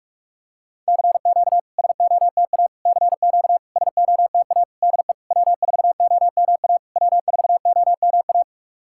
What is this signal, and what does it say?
Morse code, heard as a keyed tone, about 44 wpm, 700 Hz. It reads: CQ SOTA CQ SOTA DE W4OMA W4OMA
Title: SOTA QSO Practice 45wpm